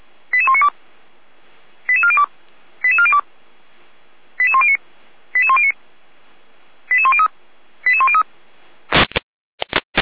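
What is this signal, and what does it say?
Five-tone selective call (ZVEI), heard as quick runs of five tones, tones of about 70 ms. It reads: 88192 88402 88402 88109 88109 88103 88103
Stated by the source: ZVEI five-tone selective call 88102 received by SDR, from the feuerwehr-orlamuende.org fire brigade server.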